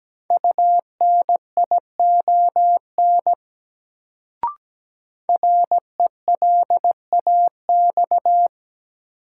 Morse code, heard as a keyed tone, about 17 wpm, 700 Hz. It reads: UNION RELAX
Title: Sentences from Top 1400 Words - New Word Drill - 17wpm